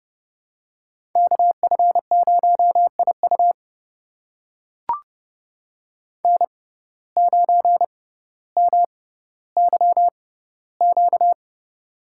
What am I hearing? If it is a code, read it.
Morse code, heard as a keyed tone, about 32 wpm, spaced as for 16 wpm, 700 Hz. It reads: KF0IU N9MYQ
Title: USA Call Signs - 5 Characters - 6x Character Spacing - 30wpm